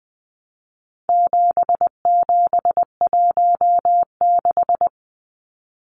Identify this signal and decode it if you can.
Morse code, heard as a keyed tone, about 20 wpm, 700 Hz. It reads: 7716
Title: Numbers - 4-Digit 20wpm